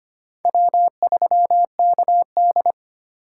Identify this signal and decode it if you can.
Morse code, heard as a keyed tone, about 25 wpm, 700 Hz. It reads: W3KD
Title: USA Call Signs - 4 Characters - 4x Character Spacing - 25wpm